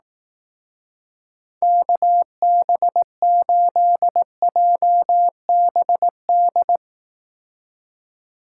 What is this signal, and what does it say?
Morse code, heard as a keyed tone, about 19 wpm, 700 Hz. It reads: KB8JBD
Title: Speed Racing - USA 2x3 Call Signs 15wpm